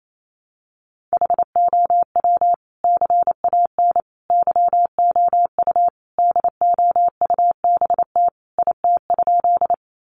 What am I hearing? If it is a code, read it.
Morse code, heard as a keyed tone, about 28 wpm, 700 Hz. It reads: HOW CAN YOU DOUBT IT?
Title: Sentences from Top 1500 Words 28wpm